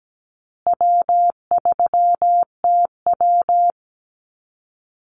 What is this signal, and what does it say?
Morse code, heard as a keyed tone, about 17 wpm, 700 Hz. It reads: W3TW
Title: Easy Call Signs Repeated 3x - 17wpm